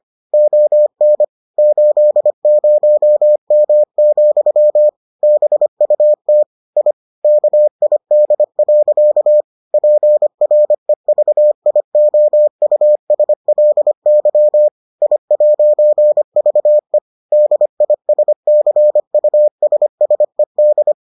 Morse code, heard as a keyed tone, about 25 wpm, 600 Hz. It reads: ON 80M, BUT I KID. PREVIOUSLY I'VE DISCUSSED